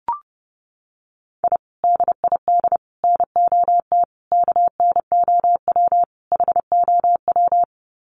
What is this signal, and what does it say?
Morse code, heard as a keyed tone, about 30 wpm, 700 Hz. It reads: I DID NOT KNOW HOW
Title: Sets of 5 Words Encore 30wpm